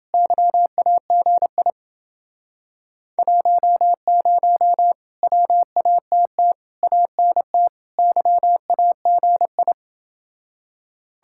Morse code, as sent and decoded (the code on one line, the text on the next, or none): -.-- .- --. .. / .---- ----- / .-- .- - - / .- -. - / -.-- .- --. ..
YAGI 10 WATT ANT YAGI